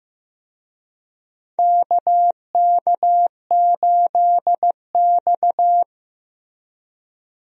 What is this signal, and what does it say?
Morse code, heard as a keyed tone, about 15 wpm, 700 Hz. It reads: KK8X
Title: USA Call Signs - 4 Characters - Repeated 2x - 15wpm